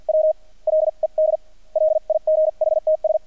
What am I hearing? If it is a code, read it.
Morse code, heard as a keyed tone, about 48 wpm, spaced as for 41 wpm, 620 Hz. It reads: W LED LIGHTS
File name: archived / CRSnet